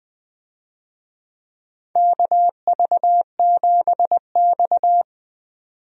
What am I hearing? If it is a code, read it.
Morse code, heard as a keyed tone, about 20 wpm, 700 Hz. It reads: KV7X
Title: USA Call Signs - 4 Characters - 20wpm